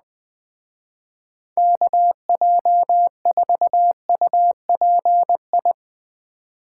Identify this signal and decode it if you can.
Morse code, heard as a keyed tone, about 19 wpm, 700 Hz. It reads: KJ4UPI
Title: Speed Racing - USA 2x3 Call Signs 17wpm